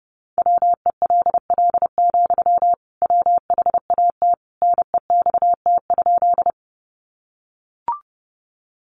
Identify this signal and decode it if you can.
Morse code, heard as a keyed tone, about 30 wpm, 700 Hz. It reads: WELL, WHAT NEXT?
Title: Sentences from Top 300 Words 30wpm